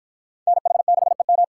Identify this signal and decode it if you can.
Morse code, heard as a keyed tone, about 48 wpm, 700 Hz. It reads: NH6EM